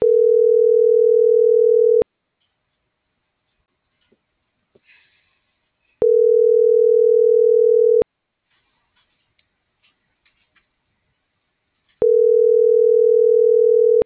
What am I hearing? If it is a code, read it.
Ringback tone (North America), 440 + 480 Hz.